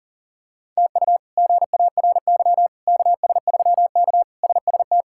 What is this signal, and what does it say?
Morse code, heard as a keyed tone, about 40 wpm, 700 Hz. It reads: TU GARY KS3K SST
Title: K1USN SST QSO Practice - 40wpm